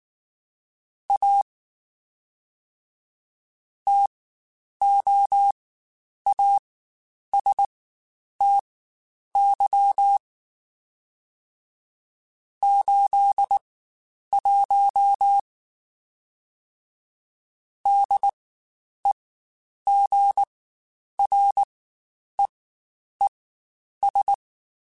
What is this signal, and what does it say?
Morse code, heard as a keyed tone, about 19 wpm, spaced as for 9 wpm, 800 Hz. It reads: A TOASTY 81 DEGREES
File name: Spaced Characters / Spaced Characters-mp3s / Weather5 12 sp @18